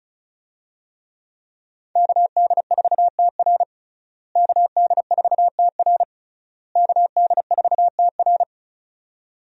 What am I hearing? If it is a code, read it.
Morse code, heard as a keyed tone, about 36 wpm, 700 Hz. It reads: KD4TR KD4TR KD4TR
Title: USA Call Signs - 5 Characters - Repeated 3x - 35wpm